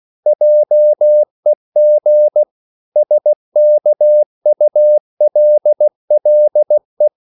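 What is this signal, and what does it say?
Morse code, heard as a keyed tone, about 16 wpm, 600 Hz. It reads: JEG SKULLE
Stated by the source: Slow (QRS) Morse code at 16 wpm, from CW med Gnister.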